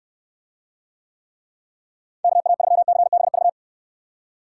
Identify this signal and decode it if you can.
Morse code, heard as a keyed tone, about 64 wpm, 700 Hz. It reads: KA3CBV